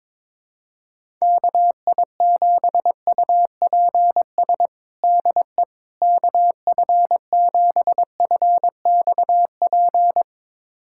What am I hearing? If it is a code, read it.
Morse code, heard as a keyed tone, about 21 wpm, 700 Hz. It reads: KI7UPS DE KF7FXP
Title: USA Callsign DE Callsign - 6 Characters - 22wpm